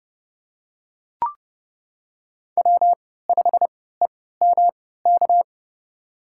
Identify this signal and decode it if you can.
Morse code, heard as a keyed tone, about 30 wpm, spaced as for 17 wpm, 700 Hz. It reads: W5EMK
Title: USA Call Signs - 5 Characters - 3x Character Spacing - 30wpm